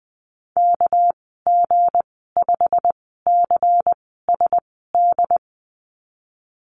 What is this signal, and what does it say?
Morse code, heard as a keyed tone, about 19 wpm, spaced as for 14 wpm, 700 Hz. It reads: KG5CSD
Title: 231 Lesson 37g - Letter x - Call Signs 20wpm